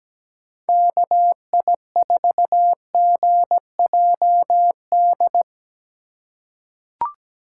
Morse code, heard as a keyed tone, about 17 wpm, 700 Hz. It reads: KI4GJD
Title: USA Call Signs - 6 Characters - 4x Character Spacing - 17wpm